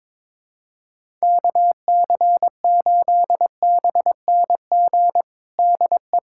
Morse code, heard as a keyed tone, about 22 wpm, 700 Hz. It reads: KC8BNG DE
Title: USA Callsign DE Callsign - 6 Characters - 22wpm